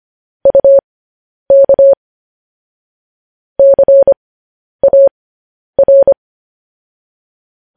Morse code, heard as a keyed tone, about 25 wpm, spaced as for 10 wpm, 550 Hz. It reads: UK CAR
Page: news headlines in morse code at 10 wpm